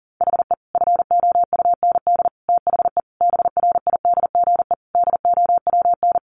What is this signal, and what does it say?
Morse code, heard as a keyed tone, about 40 wpm, 700 Hz. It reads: HE FOUND THE BRIDGE DOWN
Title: Sentences from Top 1200 Words 40wpm